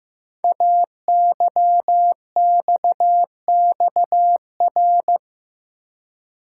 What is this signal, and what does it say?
Morse code, heard as a keyed tone, about 15 wpm, 700 Hz. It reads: AYXXR
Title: Code Groups - 5 Letters - 15wpm